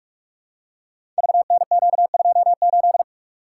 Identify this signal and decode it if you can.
Morse code, heard as a keyed tone, about 44 wpm, 690 Hz. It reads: VNQ28
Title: Code Groups - 5 Letter:Numbers - 45wpm